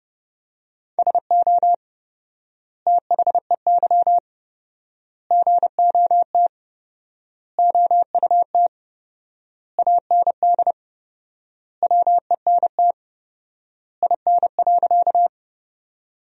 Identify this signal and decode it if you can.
Morse code, heard as a keyed tone, about 31 wpm, 700 Hz. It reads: SO THEY GOT OUT AND WENT IN.
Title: 4x Word-Spacing - Sentences from Top 300 Words - 30wpm